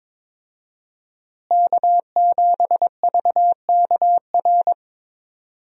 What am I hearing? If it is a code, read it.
Morse code, heard as a keyed tone, about 22 wpm, 700 Hz. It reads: K7VKR
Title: USA Call Signs - 5 Characters - Repeated 2x - 22wpm